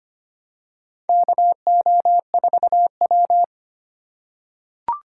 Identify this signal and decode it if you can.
Morse code, heard as a keyed tone, about 25 wpm, 700 Hz. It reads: KO4W
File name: practice / 25wpm / Four Letter-Number Combinations - 25wpm